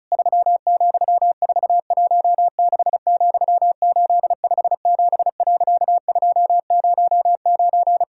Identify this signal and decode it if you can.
Morse code, heard as a keyed tone, about 36 wpm, 700 Hz. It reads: 3,416,857.209
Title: Pangram Challenge v2 - 35wpm